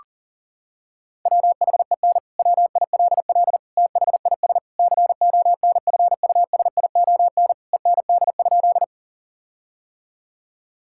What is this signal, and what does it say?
Morse code, heard as a keyed tone, about 37 wpm, 700 Hz. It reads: WHEN WILL THIS CONFUSION END?
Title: Sentences from Top 1600 Words 40wpm